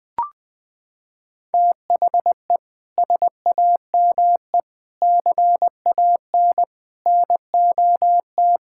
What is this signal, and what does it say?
Morse code, heard as a keyed tone, about 20 wpm, 700 Hz. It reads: THE SAME CAN NOT
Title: Sentences with 8-Word Phrases - 20wpm